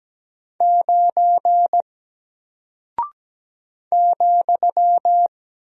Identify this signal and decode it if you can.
Morse code, heard as a keyed tone, about 17 wpm, 700 Hz. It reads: 9 ,
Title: Single Character (Letter:Number:Punctuation:Prosign) Alt 17wpm